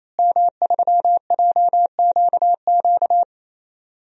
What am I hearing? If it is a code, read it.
Morse code, heard as a keyed tone, about 27 wpm, 700 Hz. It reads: M3JQQ
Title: DXCC-100 Call Signs - 5 Characters - Repeated 3x - 28wpm